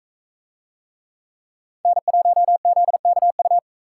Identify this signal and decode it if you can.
Morse code, heard as a keyed tone, about 44 wpm, 700 Hz. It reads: N1ZKU